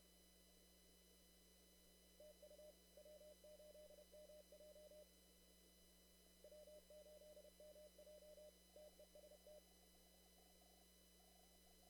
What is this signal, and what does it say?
Morse code, heard as a keyed tone, about 31 wpm, 600 Hz. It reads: TU W8MJ W8MJ TEST
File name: CW / kt5j wpx cw-op-k5pi